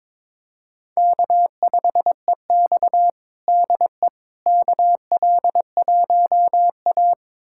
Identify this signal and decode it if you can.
Morse code, heard as a keyed tone, about 22 wpm, 700 Hz. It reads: K5EX DE KL1A
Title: USA Callsign DE Callsign - 4 Characters - 22wpm